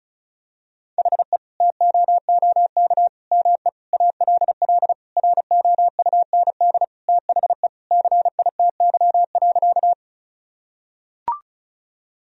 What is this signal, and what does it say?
Morse code, heard as a keyed tone, about 34 wpm, 700 Hz. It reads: HE TOOK ME ALL ROUND THE CITY.
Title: Sentences from Top 1000 Words - Review Part 1 - 35wpm